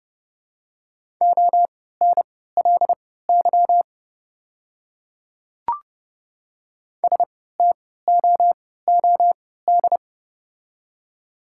Morse code, heard as a keyed tone, about 29 wpm, spaced as for 17 wpm, 700 Hz. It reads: ONLY STOOD
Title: 3x Character Spacing - Top 500 Words - 30wpm